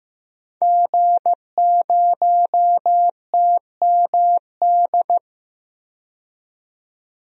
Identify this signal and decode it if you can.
Morse code, heard as a keyed tone, about 15 wpm, 700 Hz. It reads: G0TMD
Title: DXCC-100 Call Signs - 5 Characters - 15wpm